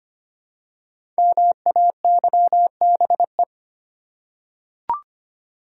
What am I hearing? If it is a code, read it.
Morse code, heard as a keyed tone, about 25 wpm, 700 Hz. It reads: MAYBE
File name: 5 Letter Words - First 3 Letters Given - 25wpm